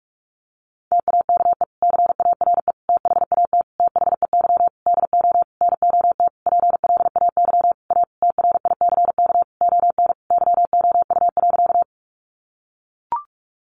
Morse code, heard as a keyed tone, about 44 wpm, 700 Hz. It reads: TAKE CARE THAT THEY DO NOT PLAY A TRICK ON YOU.
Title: Sentences from Top 1500 Words 45wpm